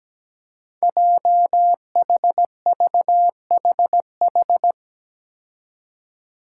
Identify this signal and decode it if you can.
Morse code, heard as a keyed tone, about 17 wpm, 700 Hz. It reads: JHVHH
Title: Five Challenging Letter-Number Combinations 17wpm